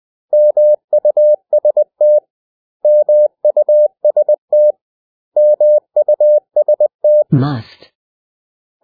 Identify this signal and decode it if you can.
Morse code, heard as a keyed tone, about 20 wpm, 600 Hz. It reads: MUST MUST MUST
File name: cw / 100 Common English Words